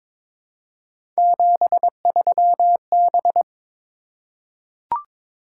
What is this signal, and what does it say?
Morse code, heard as a keyed tone, about 22 wpm, 700 Hz. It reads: 73B
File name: practice / 22wpm / Three Letter-Number Combinations - 22wpm